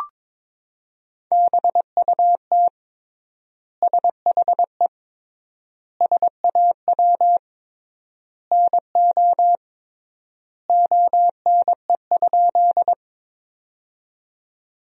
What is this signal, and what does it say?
Morse code, heard as a keyed tone, about 21 wpm, 700 Hz. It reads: BUT SHE SAW NO ONE?
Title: 3x Word-Spacing - Sentences from Top 300 Words - 22wpm